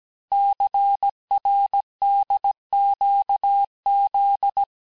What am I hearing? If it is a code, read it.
Morse code, heard as a keyed tone, about 17 wpm, 800 Hz. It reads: CRDQZ
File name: cours-du-lundi files / cours-1587-texte-chiffre-vitesse-16MPM